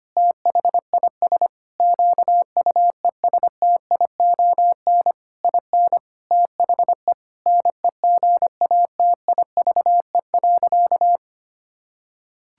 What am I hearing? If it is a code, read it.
Morse code, heard as a keyed tone, about 25 wpm, 700 Hz. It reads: THIS QUESTION IN THE NEGATIVE.
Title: Sentences from Top 1000 Words - Review Part 6 - 25wpm